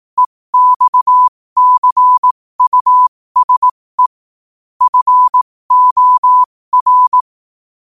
Morse code, heard as a keyed tone, about 18 wpm, spaced as for 15 wpm, 1000 Hz. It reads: EXCUSE FOR
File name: qotdfiles / 15wpm